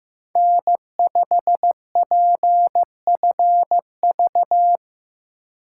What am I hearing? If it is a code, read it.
Morse code, heard as a keyed tone, about 15 wpm, 700 Hz. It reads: N5PFV
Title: USA Call Signs - 2x Character Spacing - 15wpm